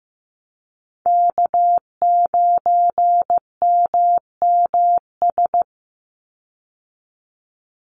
Morse code, heard as a keyed tone, about 15 wpm, 700 Hz. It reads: K9MMS
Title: Call Signs Repeated 2x - 15wpm